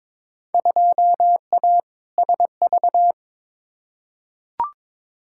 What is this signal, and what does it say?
Morse code, heard as a keyed tone, about 22 wpm, 700 Hz. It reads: 2A SV
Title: Field Day Exchanges - 22wpm